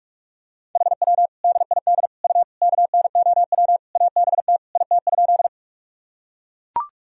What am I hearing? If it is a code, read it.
Morse code, heard as a keyed tone, about 42 wpm, 700 Hz. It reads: HW DID U KNOW ABT IT?
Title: Sentences from Top 100 Words and Abbreviations 45wpm